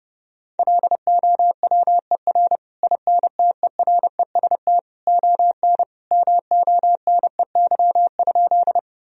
Morse code, heard as a keyed tone, about 32 wpm, 700 Hz. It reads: LOWER INTEREST ON MONEY?
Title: Sentences from Top 1000 Words - Review Part 4 - 30wpm